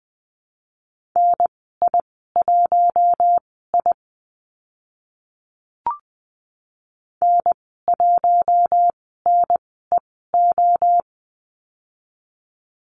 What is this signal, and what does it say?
Morse code, heard as a keyed tone, about 20 wpm, spaced as for 14 wpm, 700 Hz. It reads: NI1I N1NEO
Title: 042 Lesson 8g - Number 1 - Call Signs 20wpm